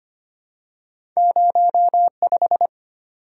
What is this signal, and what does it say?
Morse code, heard as a keyed tone, about 25 wpm, 700 Hz. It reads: 05
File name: Two Letter-Number Combinations - 25wpm